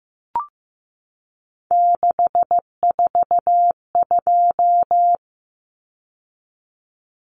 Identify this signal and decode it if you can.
Morse code, heard as a keyed tone, about 15 wpm, 700 Hz. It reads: 642
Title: Numbers - 3-Digit 15wpm